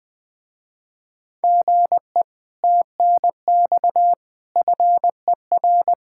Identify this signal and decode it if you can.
Morse code, heard as a keyed tone, about 20 wpm, 700 Hz. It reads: GE TNX FER
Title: Speed-Racing Common QSO Phrases Encore - 20wpm